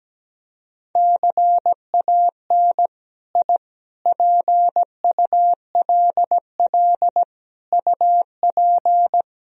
Morse code, heard as a keyed tone, about 17 wpm, 700 Hz. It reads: CAN I PULL UP
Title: Sentences from Top 1000 Words - Review Part 2 - 17wpm